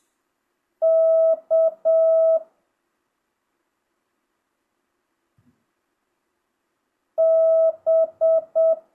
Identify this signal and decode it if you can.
Morse code, heard as a keyed tone, about 7 wpm, 650 Hz. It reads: K B